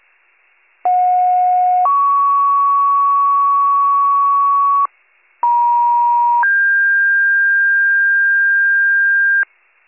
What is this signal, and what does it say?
Two-tone page, heard as a one-second tone, then a three-second tone, about 730 then 1120 Hz, about 950 then 1600 Hz, with noise. It is heard from 0.8 to 9.4 s.